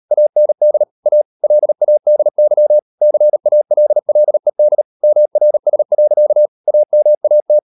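Morse code, heard as a keyed tone, about 38 wpm, 600 Hz. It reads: AND A LADY CALLED MRS. AMAT
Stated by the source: Morse code practice at 38 WPM, sending Ken Follett's novel Never in English, from CW med Gnister.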